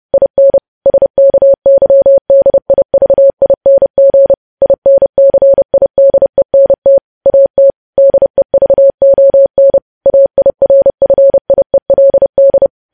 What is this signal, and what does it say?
Morse code, heard as a keyed tone, about 30 wpm, 550 Hz. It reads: IN SKYDIVING INCIDENT AT DEVON AIRFIELD
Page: News Headlines in Morse Code at 30 WPM